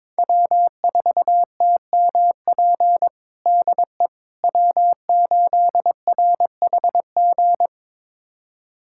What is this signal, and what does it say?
Morse code, heard as a keyed tone, about 22 wpm, 700 Hz. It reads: W4TMP DE W8RHG
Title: USA Callsign DE Callsign - 5 Characters - 22wpm